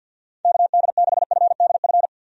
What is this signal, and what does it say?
Morse code, heard as a keyed tone, about 46 wpm, 700 Hz. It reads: KD6FDF
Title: USA Call Signs - 6 Characters - 50wpm